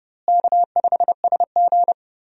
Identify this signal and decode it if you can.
Morse code, heard as a keyed tone, about 30 wpm, 700 Hz. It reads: K5SG
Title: USA Call Signs - 4 Characters - Repeated 3x - 30wpm